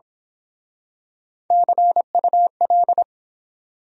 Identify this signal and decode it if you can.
Morse code, heard as a keyed tone, about 27 wpm, 700 Hz. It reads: CUL
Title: Speed-Racing Abbreviations - 22wpm